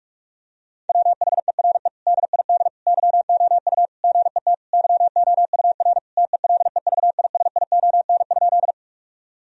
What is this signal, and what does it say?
Morse code, heard as a keyed tone, about 41 wpm, 700 Hz. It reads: WHERE DID YOU GET YOUR TELEVISION?
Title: Sentences from Top 1000 Words - Review Part 4 - 45wpm